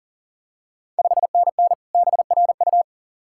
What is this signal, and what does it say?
Morse code, heard as a keyed tone, about 42 wpm, 690 Hz. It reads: 5NN BRU
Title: Ontario QSO Party Exchange 40wpm